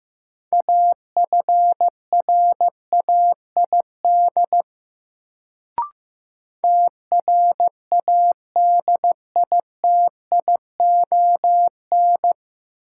Morse code, heard as a keyed tone, about 15 wpm, 700 Hz. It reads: AFRAID TRADITION
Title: Sentences from Top 1500 Words - New Word Drill Alt - 15wpm